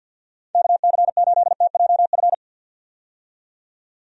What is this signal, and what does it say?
Morse code, heard as a keyed tone, about 51 wpm, 700 Hz. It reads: KC8TJF
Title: USA Call Signs - 6 Characters - 50wpm